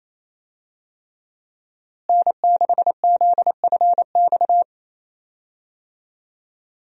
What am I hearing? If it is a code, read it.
Morse code, heard as a keyed tone, about 28 wpm, 700 Hz. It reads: N6ZFX
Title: USA Call Signs - 5 Characters - Repeated 3x - 28wpm